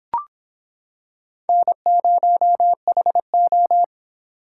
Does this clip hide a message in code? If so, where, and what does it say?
Yes: Morse code from 1.5 to 3.8 s, heard as a keyed tone, about 26 wpm, 700 Hz. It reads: N0HO